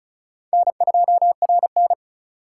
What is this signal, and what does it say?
Morse code, heard as a keyed tone, about 36 wpm, 700 Hz. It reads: N2RN
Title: USA Call Signs - 4 Characters - 2x Character Spacing - 35wpm